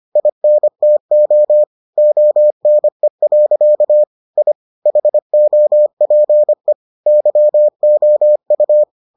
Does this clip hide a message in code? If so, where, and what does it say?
Yes: Morse code from 0.2 to 8.8 s, heard as a keyed tone, about 25 wpm, 600 Hz. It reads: INTO ONE. I HOPE YOU